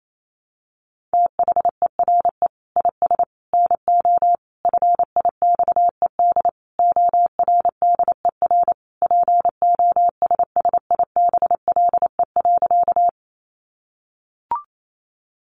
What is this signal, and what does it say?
Morse code, heard as a keyed tone, about 28 wpm, 700 Hz. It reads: THERE IS NO FIXED ORDER POSSIBLE.
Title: Sentences from Top 1100 Words 28wpm